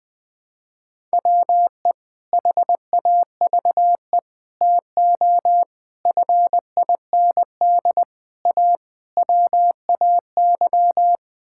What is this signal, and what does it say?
Morse code, heard as a keyed tone, about 20 wpm, 700 Hz. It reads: WE HAVE TO FIND A WAY
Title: Sentences with 7-Word Phrases - 20wpm